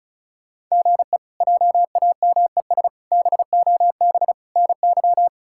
Morse code, heard as a keyed tone, about 34 wpm, 700 Hz. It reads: GE JAMES BOB NY
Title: K1USN SST QSO Practice - 35wpm